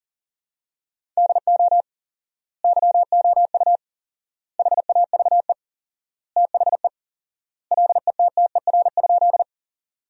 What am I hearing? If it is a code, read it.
Morse code, heard as a keyed tone, about 40 wpm, 700 Hz. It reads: DO YOU HAVE THE LETTER?
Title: 4x Word-Spacing - Sentences from Top 300 Words - 40wpm